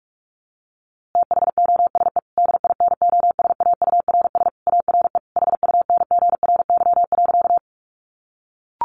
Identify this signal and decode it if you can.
Morse code, heard as a keyed tone, about 44 wpm, 700 Hz. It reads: THOSE DINOSAURS ARE HUNGRY.
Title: A Sound Of Thunder 45wpm